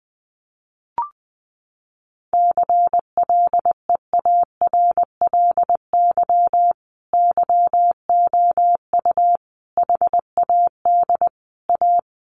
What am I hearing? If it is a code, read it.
Morse code, heard as a keyed tone, about 20 wpm, 700 Hz. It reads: CLEARLY YOU HAD A
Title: Sentences from Top 1300 Words 20wpm